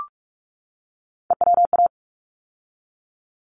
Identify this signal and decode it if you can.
Morse code, heard as a keyed tone, about 44 wpm, 700 Hz. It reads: EWA
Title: ARRL Sections 45wpm